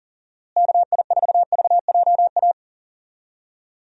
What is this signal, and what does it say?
Morse code, heard as a keyed tone, about 37 wpm, 700 Hz. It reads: KI4VJA
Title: USA Call Signs - 6 Characters - Repeated 2x - 40wpm